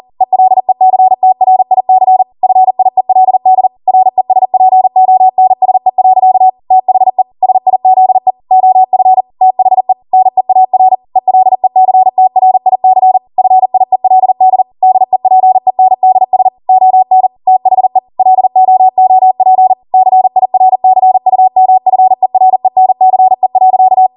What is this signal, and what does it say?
Morse code, heard as a keyed tone, about 40 wpm, 750 Hz. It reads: ELECTRIC FIELD RESPONSE. THE SIZE OF THE NEAR ELECTRIC FIELD DEPENDS ON THE LOOP CIRCUMFERENCE.
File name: Morse / 40WPM